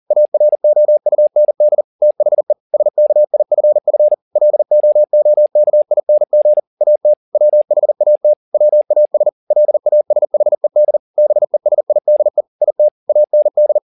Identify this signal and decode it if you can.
Morse code, heard as a keyed tone, about 40 wpm, 600 Hz. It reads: AROUND THE SKIFF LOOKING AT WHAT WAS LASHED BESIDE IT AND
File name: the old man and the sea 2 40wpm